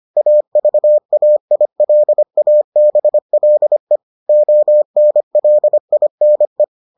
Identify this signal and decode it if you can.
Morse code, heard as a keyed tone, about 25 wpm, 600 Hz. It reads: AVAILABLE ONLINE